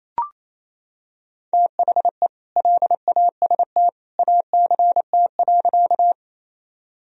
Morse code, heard as a keyed tone, about 28 wpm, 700 Hz. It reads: THE LAST ACT.
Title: Sentences from Top 1000 Words - Review Part 1 - 28wpm